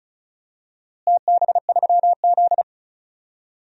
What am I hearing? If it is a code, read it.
Morse code, heard as a keyed tone, about 34 wpm, 700 Hz. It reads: TB3Z
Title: DXCC-100 Call Signs - 4 Characters - Repeated 3x - 35wpm